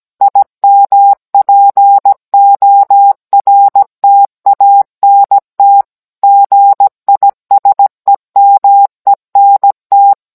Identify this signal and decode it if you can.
Morse code, heard as a keyed tone, about 17 wpm, 800 Hz. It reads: IMPORTANT GISEMENT